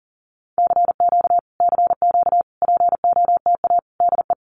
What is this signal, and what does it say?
Morse code, heard as a keyed tone, about 40 wpm, 700 Hz. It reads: CQ CQ POTA DE
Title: Common QSO Phrases Encore 40wpm